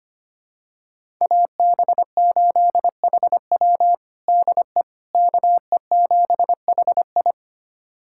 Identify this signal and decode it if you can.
Morse code, heard as a keyed tone, about 25 wpm, 700 Hz. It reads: AB8HW DE KE7HI
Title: USA Callsign DE Callsign - 5 Characters - 25wpm